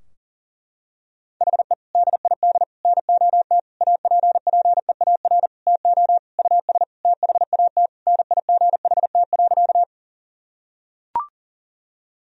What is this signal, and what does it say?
Morse code, heard as a keyed tone, about 40 wpm, 700 Hz. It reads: HE DID NOT APPEAR TO US THAT NIGHT.
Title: Sentences from Top 1000 Words - Review Part 2 - 40wpm